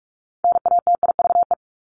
Morse code, heard as a keyed tone, about 44 wpm, 700 Hz. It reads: NATIVE
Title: Top 1751-2000 Words 45wpm